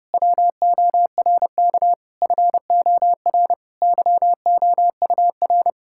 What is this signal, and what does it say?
Morse code, heard as a keyed tone, about 32 wpm, 700 Hz. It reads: WORK FOR YOUR
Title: Sentences from Top 1000 Words - Review Part 6 - 30wpm